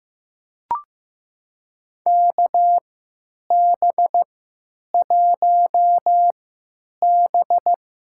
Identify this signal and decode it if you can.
Morse code, heard as a keyed tone, about 15 wpm, spaced as for 9 wpm, 700 Hz. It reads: KB1B
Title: USA Call Signs - 6 Characters - 3x Character Spacing - 15wpm